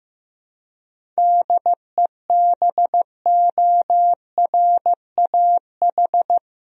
Morse code, heard as a keyed tone, about 15 wpm, 700 Hz. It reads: DEBORAH
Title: USA Common Names - 15wpm